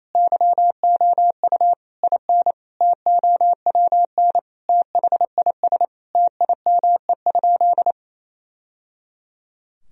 Morse code, heard as a keyed tone, about 28 wpm, 700 Hz. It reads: YOU IN TOWN THIS TIME?
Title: Sentences from Top 1000 Words - Review Part 2 - 28wpm